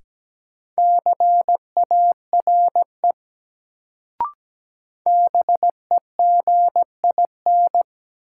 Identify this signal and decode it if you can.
Morse code, heard as a keyed tone, about 17 wpm, 700 Hz. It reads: CARE BEGIN
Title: Sentences from Top 300 Words - New Word Drill - 17wpm